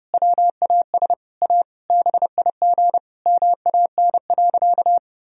Morse code, heard as a keyed tone, about 30 wpm, 700 Hz. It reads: WAS A BIG MAN.
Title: Sentences from Top 1000 Words - Review Part 1 - 30wpm